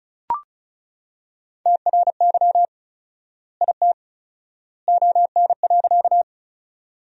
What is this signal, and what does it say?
Morse code, heard as a keyed tone, about 34 wpm, 700 Hz. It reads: TRY IT ON.
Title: 4x Word-Spacing - Sentences from Top 200 Words - 35wpm